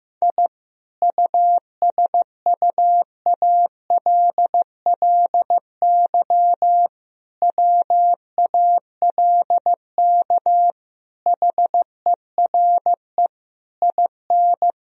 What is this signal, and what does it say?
Morse code, heard as a keyed tone, about 15 wpm, 700 Hz. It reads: I USUALLY WALK HERE IN
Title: Sentences from Top 1000 Words - Review Part 3 - 15wpm